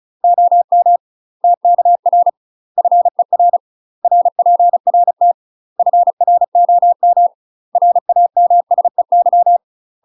Morse code, heard as a keyed tone, about 34 wpm, 700 Hz. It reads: OM TKR FER RPRT FROM RAMSEY
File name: UK MASTER QSOs TRAINING / 35WPM / UK MASTER QSOs 0034